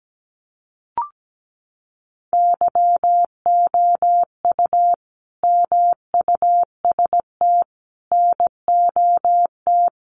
Morse code, heard as a keyed tone, about 17 wpm, 700 Hz. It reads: YOU MUST NOT
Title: Sentences from Top 600 Words 17wpm